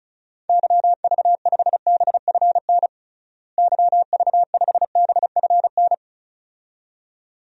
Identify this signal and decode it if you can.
Morse code, heard as a keyed tone, about 36 wpm, 700 Hz. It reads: YV5BFN YV5BFN
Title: DXCC-100 Call Signs - 6 Characters - Repeated 2x - 35wpm